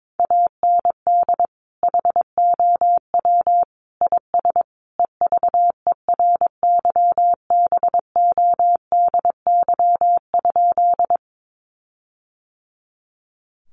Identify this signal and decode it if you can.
Morse code, heard as a keyed tone, about 22 wpm, 700 Hz. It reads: AND HOW IS EVERYBODY?
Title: Sentences from Top 1400 Words 22wpm